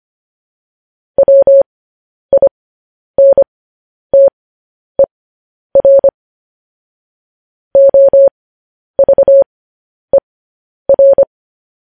Morse code, heard as a keyed tone, about 25 wpm, spaced as for 10 wpm, 550 Hz. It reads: WINTER OVER
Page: news headlines in morse code at 10 wpm